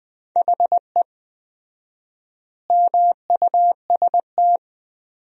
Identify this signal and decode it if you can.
Morse code, heard as a keyed tone, about 20 wpm, 700 Hz. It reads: HE MUST
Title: 4x Word-Spacing - Sentences from Top 500 Words - 20wpm